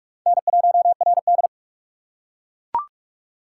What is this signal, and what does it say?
Morse code, heard as a keyed tone, about 42 wpm, 700 Hz. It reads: N1RD
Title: USA Call Signs - 4 Characters - 6x Character Spacing - 45wpm